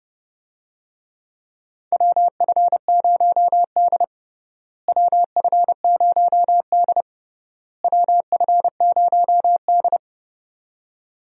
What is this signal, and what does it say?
Morse code, heard as a keyed tone, about 30 wpm, 700 Hz. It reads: WF0D WF0D WF0D
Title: USA Call Signs - 4 Characters - Repeated 3x - 30wpm